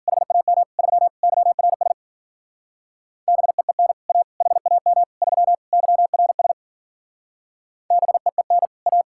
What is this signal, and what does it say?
Morse code, heard as a keyed tone, about 51 wpm, 700 Hz. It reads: HAM 3 YRS BEEN A HAM 3 YRS BEEN A